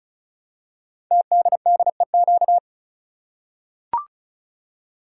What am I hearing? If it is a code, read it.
Morse code, heard as a keyed tone, about 36 wpm, 700 Hz. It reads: TDDEQ
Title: Code Groups - 5 Letters - 35wpm